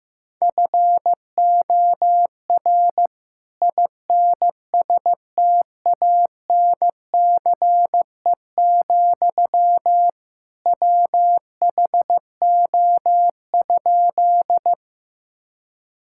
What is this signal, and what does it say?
Morse code, heard as a keyed tone, about 15 wpm, 700 Hz. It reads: FOR INSTANCE, WHO?
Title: Sentences from Top 1000 Words - Review Part 5 - 15wpm